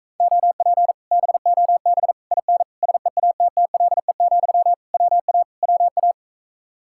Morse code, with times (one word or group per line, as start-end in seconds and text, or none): 0.2-0.9 OP
1.1-2.1 BOB
2.3-2.6 IN
2.8-4.7 SEATTLE,
4.9-5.4 WA
5.6-6.1 WA